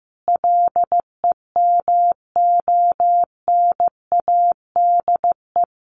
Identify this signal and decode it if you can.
Morse code, heard as a keyed tone, about 15 wpm, 700 Hz. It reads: LEMONADE
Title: Additional Easy Phrases 15wpm